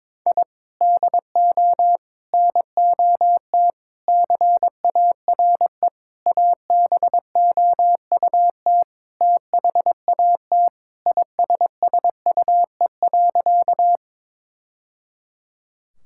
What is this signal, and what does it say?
Morse code, heard as a keyed tone, about 22 wpm, 700 Hz. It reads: I DO NOT CARE ABOUT THAT ISSUE.
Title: Sentences from Top 1000 Words - Review Part 6 - 22wpm